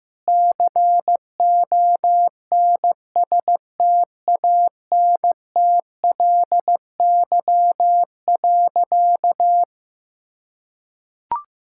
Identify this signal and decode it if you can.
Morse code, heard as a keyed tone, about 15 wpm, 700 Hz. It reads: CONSTANTLY.
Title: Sentences from Top 1600 Words 15wpm